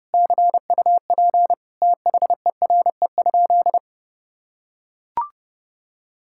Morse code, heard as a keyed tone, about 30 wpm, 700 Hz. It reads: CUP THERE?
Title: Sentences from Top 1600 Words 30wpm